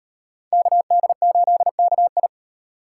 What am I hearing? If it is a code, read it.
Morse code, heard as a keyed tone, about 36 wpm, 700 Hz. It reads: KD8KI